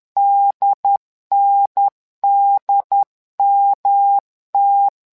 Morse code, heard as a keyed tone, about 10 wpm, 800 Hz. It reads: DNDMT
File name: LEZIONI / 3 MWRO / 3 MWRO 10wpm 2148